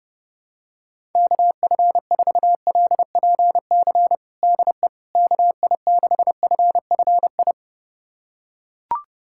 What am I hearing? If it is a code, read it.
Morse code, heard as a keyed tone, about 30 wpm, 700 Hz. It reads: KF4LPC DE KI6FFI
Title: USA Callsign DE Callsign - 6 Characters - 30wpm